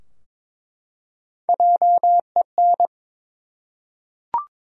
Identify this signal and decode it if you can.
Morse code, heard as a keyed tone, about 21 wpm, 700 Hz. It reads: JEN
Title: USA Common Names - 22wpm